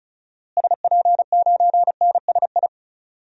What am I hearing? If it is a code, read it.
Morse code, heard as a keyed tone, about 36 wpm, 700 Hz. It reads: SP9NSI
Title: DXCC-100 Call Signs - 6 Characters - Repeated 2x - 35wpm